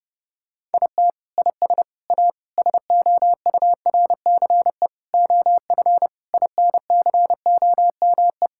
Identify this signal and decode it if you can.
Morse code, heard as a keyed tone, about 32 wpm, 700 Hz. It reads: IT IS A SOURCE OF INCOME
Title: Sentences from Top 1000 Words - Review Part 5 - 30wpm